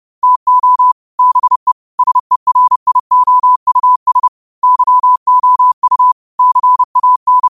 Morse code, heard as a keyed tone, about 30 wpm, 1000 Hz. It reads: TO BE SERIOUS YOU CAN
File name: qotdfiles / 30wpm